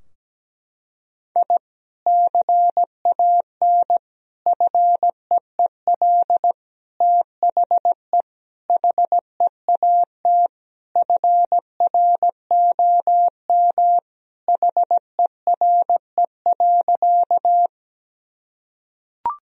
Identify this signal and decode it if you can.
Morse code, heard as a keyed tone, about 17 wpm, 700 Hz. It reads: I CAN FEEL THE HEAT FROM HERE.
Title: Sentences from Top 1000 Words - Review Part 3 - 17wpm